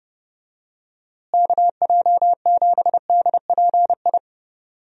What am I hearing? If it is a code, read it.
Morse code, heard as a keyed tone, about 30 wpm, 700 Hz. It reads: KJ7DPI
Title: USA Call Signs - 6 Characters - Repeated 3x - 30wpm